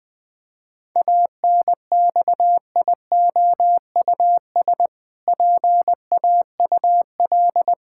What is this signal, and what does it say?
Morse code, heard as a keyed tone, about 21 wpm, 700 Hz. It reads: ANXIOUS PAUL
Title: Pangram Challenge v2 - 20wpm